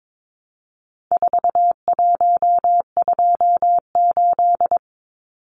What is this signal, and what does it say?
Morse code, heard as a keyed tone, about 22 wpm, 700 Hz. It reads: 4128
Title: Numbers - 4-Digit 22wpm